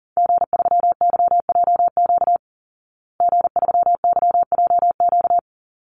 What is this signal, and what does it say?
Morse code, heard as a keyed tone, about 40 wpm, 700 Hz. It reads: G3YJQ G3YJQ
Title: Call Signs Repeated 2x - 40wpm